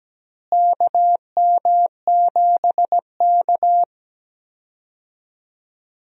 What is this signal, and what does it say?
Morse code, heard as a keyed tone, about 17 wpm, 700 Hz. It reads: KM7K
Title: USA Call Signs - 4 Characters - Repeated 3x - 17wpm